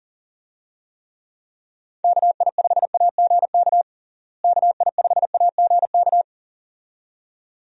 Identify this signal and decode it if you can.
Morse code, heard as a keyed tone, about 40 wpm, 700 Hz. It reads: KI5AGK KI5AGK
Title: USA Call Signs - 6 Characters - Repeated 2x - 40wpm